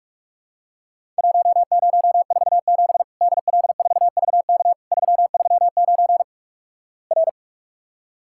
Morse code, heard as a keyed tone, about 46 wpm, 700 Hz. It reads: 1047 DL4VK 339 R
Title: Worked All Europe DX Contest - QTC Exchange - 45wpm